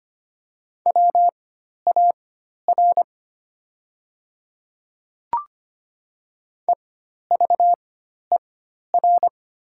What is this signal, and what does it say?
Morse code, heard as a keyed tone, about 25 wpm, spaced as for 12 wpm, 700 Hz. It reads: WAR EVER
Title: 4x Character Spacing - Top 500 Words - 25wpm